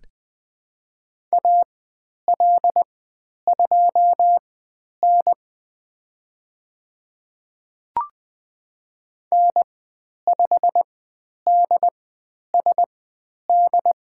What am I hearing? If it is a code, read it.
Morse code, heard as a keyed tone, about 20 wpm, spaced as for 10 wpm, 700 Hz. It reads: AL2N N5DSD
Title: 095 Lesson 16g - Letter c - Call Signs 20wpm